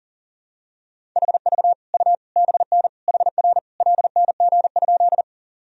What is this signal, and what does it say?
Morse code, heard as a keyed tone, about 40 wpm, 700 Hz. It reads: HV U BN HR LNG?
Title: Sentences from Top 200 Words and Abbreviations 40wpm